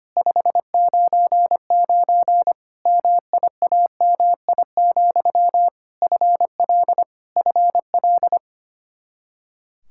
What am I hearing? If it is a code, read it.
Morse code, heard as a keyed tone, about 25 wpm, 700 Hz. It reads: 599 MIAMI, FL FL
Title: Practice QSO - Exchange 1 - Beginning - 25wpm